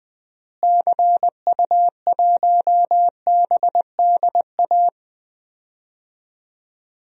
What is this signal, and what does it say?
Morse code, heard as a keyed tone, about 21 wpm, 700 Hz. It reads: CU1BDA
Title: DXCC-100 Call Signs - 6 Characters - Repeated 2x - 20wpm